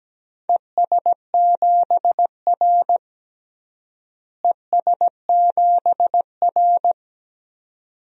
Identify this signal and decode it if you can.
Morse code, heard as a keyed tone, about 17 wpm, 700 Hz. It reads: ES7R ES7R
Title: DXCC-100 Call Signs - 4 Characters - Repeated 2x - 17wpm